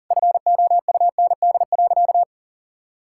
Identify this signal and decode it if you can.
Morse code, heard as a keyed tone, about 40 wpm, 700 Hz. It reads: FOUND.
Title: Sentences from Top 1600 Words 40wpm